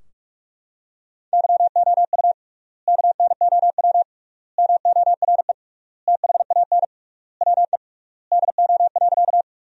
Morse code, heard as a keyed tone, about 43 wpm, 700 Hz. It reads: YOU KNOW MORE THAN WE DO.
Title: 3x Word-Spacing - Sentences from Top 100 Words - 45wpm